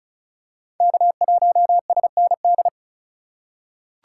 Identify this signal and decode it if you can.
Morse code, heard as a keyed tone, about 36 wpm, 700 Hz. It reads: K1SND
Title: K1USN SST QSO Practice - 35wpm